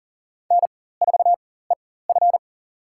Morse code, heard as a keyed tone, about 37 wpm, spaced as for 18 wpm, 700 Hz. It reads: N4EF
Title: USA Call Signs - 4 Characters - 4x Character Spacing - 40wpm